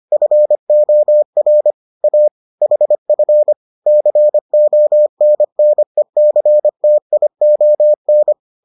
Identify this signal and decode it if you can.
Morse code, heard as a keyed tone, about 25 wpm, 600 Hz. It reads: FOR A HF CONNECTION